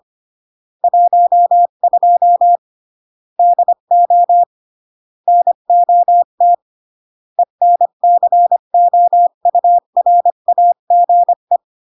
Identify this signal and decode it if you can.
Morse code, heard as a keyed tone, about 25 wpm, 700 Hz. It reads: 12 DO NOT ENCOURAGE